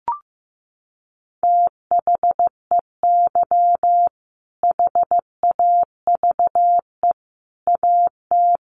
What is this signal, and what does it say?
Morse code, heard as a keyed tone, about 15 wpm, 700 Hz. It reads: THEY HAVE AT
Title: Sentences from Top 900 Words 15wpm